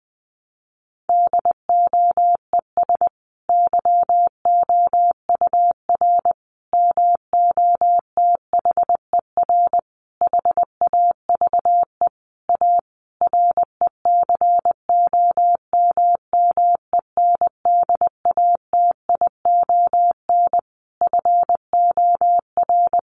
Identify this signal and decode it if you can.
Morse code, heard as a keyed tone, about 20 wpm, 700 Hz. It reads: DOES YOUR MOTHER HAVE A RECOMMENDATION FOR
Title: Sentences from Top 1400 Words 20wpm